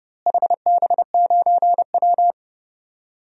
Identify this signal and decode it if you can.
Morse code, heard as a keyed tone, about 30 wpm, 700 Hz. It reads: HB9W
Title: DXCC-100 Call Signs - 4 Characters - Repeated 2x - 30wpm